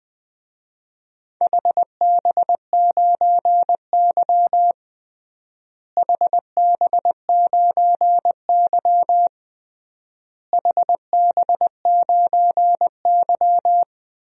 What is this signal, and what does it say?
Morse code, heard as a keyed tone, about 20 wpm, 700 Hz. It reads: HB9Y HB9Y HB9Y
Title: DXCC-100 Call Signs - 4 Characters - Repeated 3x - 20wpm